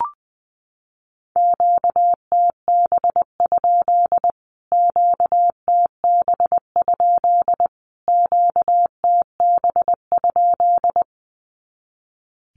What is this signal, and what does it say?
Morse code, heard as a keyed tone, about 20 wpm, 700 Hz. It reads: QTB? QTB? QTB?
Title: Q-Codes Repeated 3x 20wpm